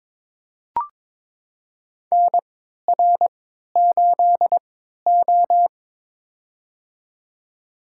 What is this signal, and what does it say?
Morse code, heard as a keyed tone, about 21 wpm, spaced as for 12 wpm, 700 Hz. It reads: NR8O
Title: USA Call Signs - 4 Characters - 3x Character Spacing - 22wpm